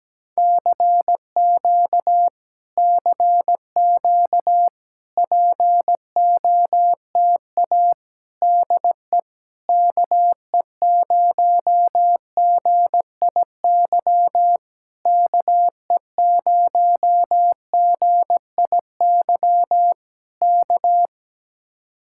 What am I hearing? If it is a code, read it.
Morse code, heard as a keyed tone, about 17 wpm, 700 Hz. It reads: CQ CQ POTA DE KE0GIY KE0GIY K
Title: Speed-Racing Common QSO Phrases Encore - 17wpm